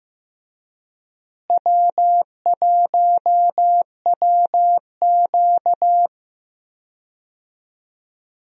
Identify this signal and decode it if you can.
Morse code, heard as a keyed tone, about 15 wpm, 700 Hz. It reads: W1WQ
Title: USA Call Signs - 4 Characters - Repeated 2x - 15wpm